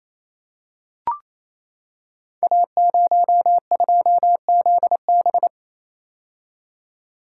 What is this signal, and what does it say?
Morse code, heard as a keyed tone, about 29 wpm, 700 Hz. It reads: A02ZB
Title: Code Groups - 5 Letter:Numbers - 28wpm